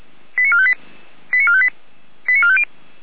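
Five-tone selective call (ZVEI), heard as quick runs of five tones, tones of about 70 ms. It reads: 89458 89458 89450
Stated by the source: ZVEI five-tone selective call received by SDR.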